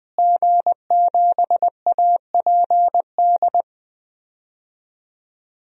Morse code, heard as a keyed tone, about 20 wpm, 700 Hz. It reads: G7APD
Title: DXCC-100 Call Signs - 5 Characters - Repeated 3x - 20wpm